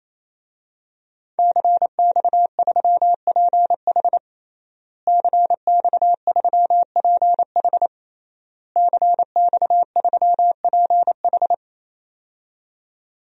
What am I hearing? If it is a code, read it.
Morse code, heard as a keyed tone, about 28 wpm, 700 Hz. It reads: CX3PH CX3PH CX3PH
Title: DXCC-100 Call Signs - 5 Characters - Repeated 3x - 28wpm